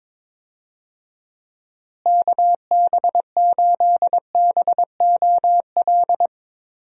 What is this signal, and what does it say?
Morse code, heard as a keyed tone, about 22 wpm, 700 Hz. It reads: KB8BOL
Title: USA Call Signs - 6 Characters - Repeated 2x - 22wpm